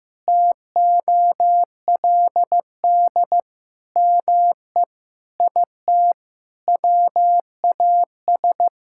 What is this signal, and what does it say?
Morse code, heard as a keyed tone, about 15 wpm, 700 Hz. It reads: TOLD ME IT WAS
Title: Sentences from Top 1600 Words 15wpm